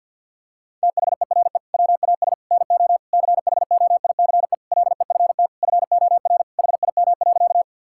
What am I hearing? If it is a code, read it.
Morse code, heard as a keyed tone, about 49 wpm, 700 Hz. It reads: THERE WAS NO CHOICE LEFT FOR HIM.
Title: Sentences from Top 1000 Words - Review Part 6 - 50wpm